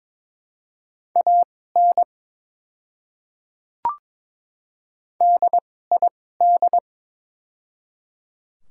Morse code, heard as a keyed tone, about 22 wpm, spaced as for 16 wpm, 700 Hz. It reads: AN DID
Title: 2x Character Spacing - Top 100 Words - 22wpm